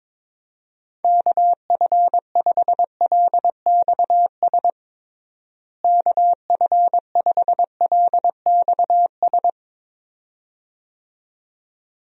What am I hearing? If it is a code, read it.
Morse code, heard as a keyed tone, about 22 wpm, 700 Hz. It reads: KF5LXS KF5LXS
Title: USA Call Signs - 6 Characters - Repeated 2x - 22wpm